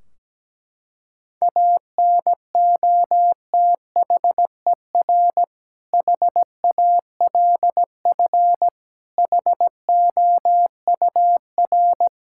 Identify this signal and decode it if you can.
Morse code, heard as a keyed tone, about 17 wpm, 700 Hz. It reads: ANOTHER HALF HOUR
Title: Sentences from Top 1600 Words 17wpm